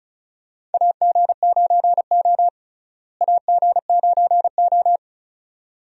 Morse code, heard as a keyed tone, about 36 wpm, 700 Hz. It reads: AG9O AG9O
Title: USA Call Signs - 4 Characters - Repeated 2x - 35wpm